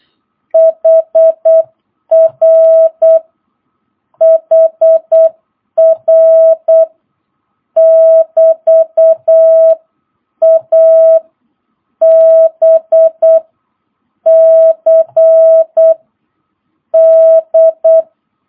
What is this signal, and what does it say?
Morse code, heard as a keyed tone, about 8 wpm, 650 Hz. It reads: HR HR <BT> A B C D
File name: jh8cbh / 2022cw